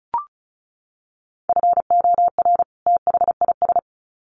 Morse code, heard as a keyed tone, about 34 wpm, 700 Hz. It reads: FOR THIS
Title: Sets of 2 Words 35wpm